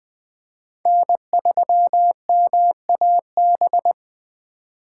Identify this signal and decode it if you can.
Morse code, heard as a keyed tone, about 20 wpm, 700 Hz. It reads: N3MAB
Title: USA Call Signs - 5 Characters - 4x Character Spacing - 20wpm